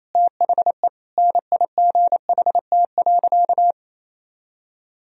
Morse code, heard as a keyed tone, about 28 wpm, 700 Hz. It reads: THE NIGHT.
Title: Sentences from Top 1000 Words - Review Part 2 - 28wpm